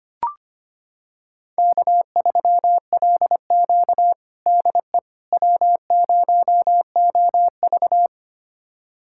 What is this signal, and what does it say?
Morse code, heard as a keyed tone, about 25 wpm, 700 Hz. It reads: K3LQ DE W0OV
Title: USA Callsign DE Callsign - 4 Characters - 25wpm